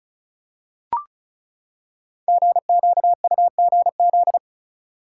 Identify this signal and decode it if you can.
Morse code, heard as a keyed tone, about 36 wpm, 700 Hz. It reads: GQUGZ
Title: Code Groups - 5 Letters - 35wpm